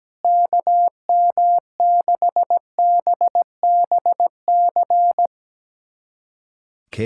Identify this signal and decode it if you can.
Morse code, heard as a keyed tone, about 17 wpm, 700 Hz. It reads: KM6BBC
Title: USA Callsign DE Callsign 17wpm